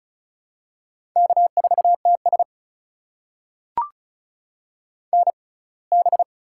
Morse code, heard as a keyed tone, about 36 wpm, spaced as for 24 wpm, 700 Hz. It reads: K4TS NB
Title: USA Call Signs - 4 Characters - 6x Character Spacing - 35wpm